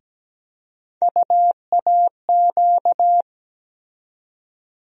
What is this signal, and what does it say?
Morse code, heard as a keyed tone, about 17 wpm, 700 Hz. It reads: UAQ
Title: USA Call Sign Postfixes - 3 Letters 17wpm